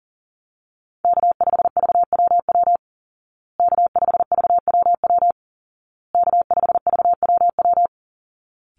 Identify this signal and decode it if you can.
Morse code, heard as a keyed tone, about 37 wpm, 700 Hz. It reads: K5VWW K5VWW K5VWW
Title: Call Signs Repeated 2x - 40wpm